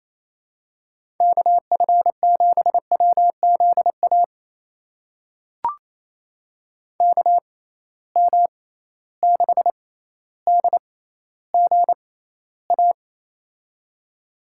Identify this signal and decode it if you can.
Morse code, heard as a keyed tone, about 27 wpm, spaced as for 14 wpm, 700 Hz. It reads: KF7WZA KM6DGA
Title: USA Call Signs - 6 Characters - 6x Character Spacing - 28wpm